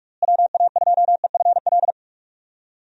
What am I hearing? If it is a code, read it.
Morse code, heard as a keyed tone, about 46 wpm, 700 Hz. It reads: WA2EFL
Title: USA Call Signs - 6 Characters - Repeated 3x - 45wpm